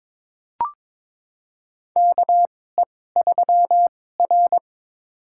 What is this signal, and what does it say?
Morse code, heard as a keyed tone, about 21 wpm, spaced as for 16 wpm, 700 Hz. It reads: KE3R
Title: USA Call Signs - 2x Character Spacing - 22wpm